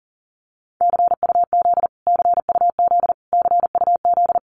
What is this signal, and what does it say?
Morse code, heard as a keyed tone, about 40 wpm, 700 Hz. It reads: CUZ CUZ CUZ
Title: Abbreviations Repeated 3x - 40wpm